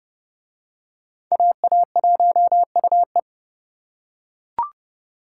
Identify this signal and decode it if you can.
Morse code, heard as a keyed tone, about 30 wpm, 700 Hz. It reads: AA1UE